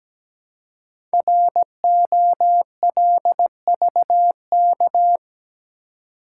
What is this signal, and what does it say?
Morse code, heard as a keyed tone, about 17 wpm, 700 Hz. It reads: ROLVK
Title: Code Groups - 5 Letter:Numbers - 17wpm